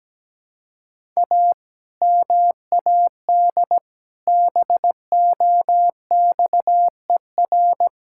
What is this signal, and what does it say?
Morse code, heard as a keyed tone, about 17 wpm, 700 Hz. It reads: A MAD BOXER
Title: Pangram Challenge v2 - 17wpm